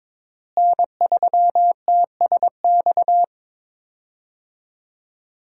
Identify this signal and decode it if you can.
Morse code, heard as a keyed tone, about 22 wpm, 700 Hz. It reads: N3TSX
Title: USA Call Signs - 5 Characters - Repeated 2x - 22wpm